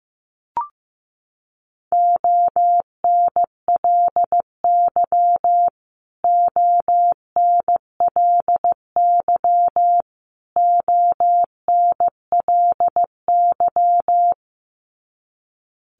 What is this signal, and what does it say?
Morse code, heard as a keyed tone, about 15 wpm, 700 Hz. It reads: ONLY ONLY ONLY
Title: Top 300 Words Repeated 15wpm